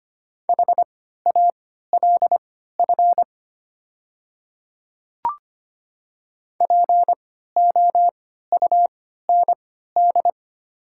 Morse code, heard as a keyed tone, about 25 wpm, spaced as for 14 wpm, 700 Hz. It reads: HALF POUND
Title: 3x Character Spacing - Top 500 Words - 25wpm